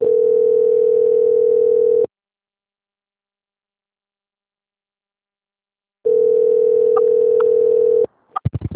Ringback tone (North America), 440 + 480 Hz.
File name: out- - - - - .7655